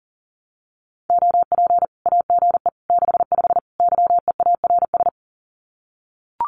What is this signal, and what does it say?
Morse code, heard as a keyed tone, about 40 wpm, 700 Hz. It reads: OP AGE 65 YEARS
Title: Common QSO Phrases 40wpm